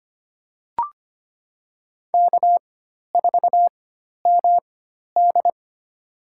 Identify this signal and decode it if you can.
Morse code, heard as a keyed tone, about 25 wpm, spaced as for 12 wpm, 700 Hz. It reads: K4MD
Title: USA Call Signs - 4 Characters - 4x Character Spacing - 25wpm